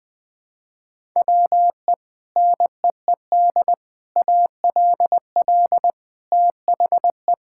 Morse code, heard as a keyed tone, about 21 wpm, 700 Hz. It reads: WE NEED ALL THE
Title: Sets of 8 Words - 20wpm